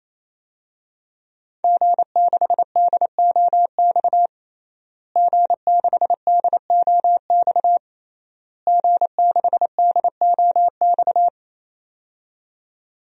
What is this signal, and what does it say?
Morse code, heard as a keyed tone, about 27 wpm, 700 Hz. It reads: G6DOX G6DOX G6DOX
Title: DXCC-100 Call Signs - 5 Characters - Repeated 3x - 28wpm